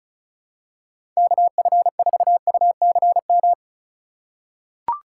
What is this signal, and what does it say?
Morse code, heard as a keyed tone, about 36 wpm, 700 Hz. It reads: KF4UCM
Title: USA Call Signs - 3x Character Spacing - 35wpm